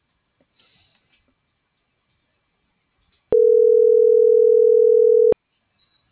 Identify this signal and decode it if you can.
Ringback tone (North America), 440 + 480 Hz.